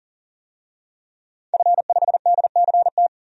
Morse code, heard as a keyed tone, about 40 wpm, 700 Hz. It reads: F5DCT
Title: DXCC-100 Call Signs - 5 Characters - 40wpm